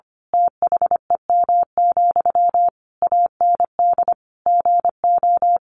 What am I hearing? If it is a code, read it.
Morse code, heard as a keyed tone, about 25 wpm, 700 Hz. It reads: THEM, AND GO
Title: A Sound Of Thunder 25wpm